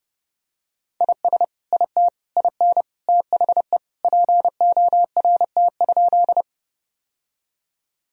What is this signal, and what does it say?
Morse code, heard as a keyed tone, about 32 wpm, 700 Hz. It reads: IS IT IN THE PORT?
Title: Sentences from Top 1000 Words - Review Part 1 - 30wpm